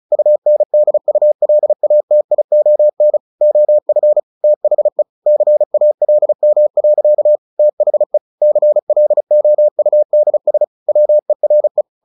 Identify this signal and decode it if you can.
Morse code, heard as a keyed tone, about 34 wpm, 600 Hz. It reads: UNDULATION OF THE CALM. THE CLOUDS WERE
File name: the old man and the sea / the old man and the sea 1 35wpm